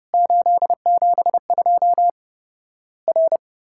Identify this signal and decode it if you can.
Morse code, heard as a keyed tone, about 30 wpm, 700 Hz. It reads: 872 R
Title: Worked All Europe DX Contest - QTC Exchange - 30wpm